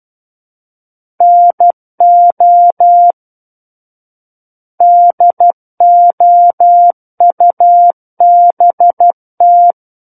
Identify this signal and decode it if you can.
Morse code, heard as a keyed tone, about 12 wpm, 700 Hz. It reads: NO DOUBT
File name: Book One / 03-Chapter 3 - 12 WPM